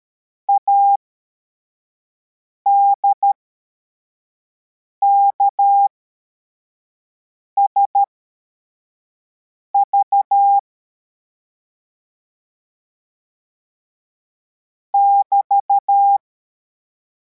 Morse code, heard as a keyed tone, about 13 wpm, spaced as for 4 wpm, 800 Hz. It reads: ADKSV <BT>
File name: formation / cw / Text55